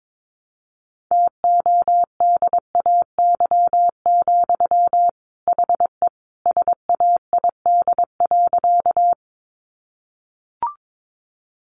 Morse code, heard as a keyed tone, about 22 wpm, 700 Hz. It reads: TODAY, HE SAID.
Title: Sentences from Top 600 Words 22wpm